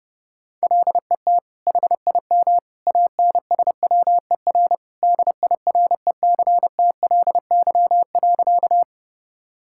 Morse code, heard as a keyed tone, about 30 wpm, 700 Hz. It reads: LET HIM ANSWER DIRECTLY.
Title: Sentences from Top 1000 Words - Review Part 5 - 30wpm